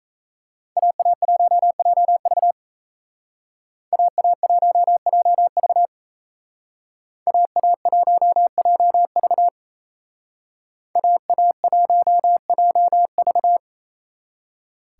Morse code, heard as a keyed tone, about 36 wpm, 700 Hz. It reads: AA1JV AA1JV AA1JV AA1JV